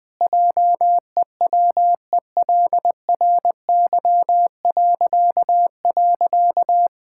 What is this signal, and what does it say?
Morse code, heard as a keyed tone, about 20 wpm, 700 Hz. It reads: JEWELRY..
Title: Pangram Challenge v2 - 20wpm